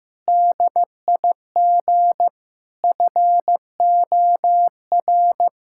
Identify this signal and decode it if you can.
Morse code, heard as a keyed tone, about 16 wpm, 700 Hz. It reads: DIG FOR
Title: Sentences from Top 1600 Words 15wpm